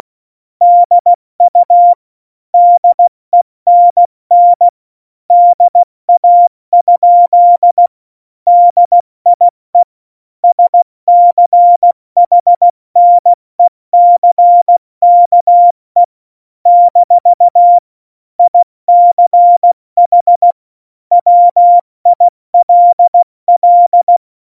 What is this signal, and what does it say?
Morse code, heard as a keyed tone, about 16 wpm, 700 Hz. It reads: DU DENN DA? DIE SCHNECKE - ICH WILL
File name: Morsekurs / Witze / Witz 14. Januar 2020, 70 F80